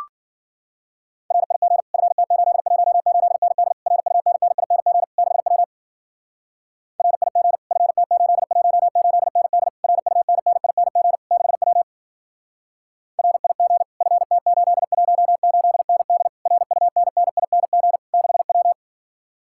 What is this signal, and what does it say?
Morse code, heard as a keyed tone, about 52 wpm, 700 Hz. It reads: RIG FT818ND RUNNING 6W RIG FT818ND RUNNING 6W RIG FT818ND RUNNING 6W